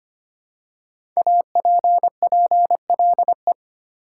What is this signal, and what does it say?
Morse code, heard as a keyed tone, about 26 wpm, 700 Hz. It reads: APPLE
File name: 5 Letter Words - First 3 Letters Given - 25wpm